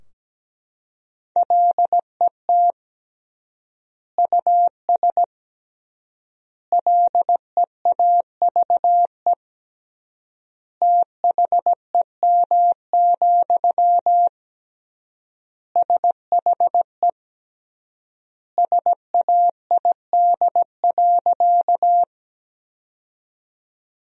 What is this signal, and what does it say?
Morse code, heard as a keyed tone, about 17 wpm, 700 Hz. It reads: LET US LEAVE THEM, SHE SAID.
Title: 3x Word-Spacing - Sentences from Top 400 Words - 17wpm